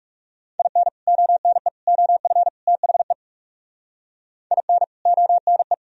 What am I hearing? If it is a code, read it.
Morse code, heard as a keyed tone, about 37 wpm, 700 Hz. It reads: IN ONE OF THE IN ONE
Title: Speed-Racing - Sets of 4 Words 30wpm